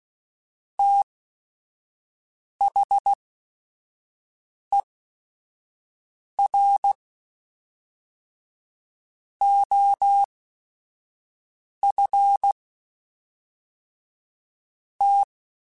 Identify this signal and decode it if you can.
Morse code, heard as a keyed tone, about 16 wpm, spaced as for 4 wpm, 800 Hz. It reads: THE ROFT
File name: Technical Subjects / Technical Subjects-mp3s / AND Gate @5